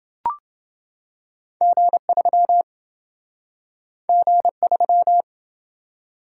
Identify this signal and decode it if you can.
Morse code, heard as a keyed tone, about 26 wpm, 700 Hz. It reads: G3 G3